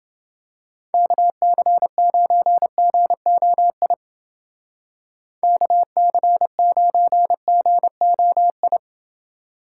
Morse code, heard as a keyed tone, about 28 wpm, 700 Hz. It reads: KC9GOI KC9GOI